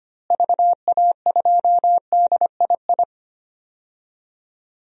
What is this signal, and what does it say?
Morse code, heard as a keyed tone, about 25 wpm, 700 Hz. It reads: VA2DII
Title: DXCC-100 Call Signs - 6 Characters - Repeated 2x - 25wpm